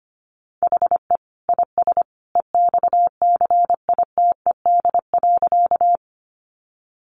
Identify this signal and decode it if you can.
Morse code, heard as a keyed tone, about 25 wpm, 700 Hz. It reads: HE IS EXCITED.
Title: Additional Easy Phrases 25wpm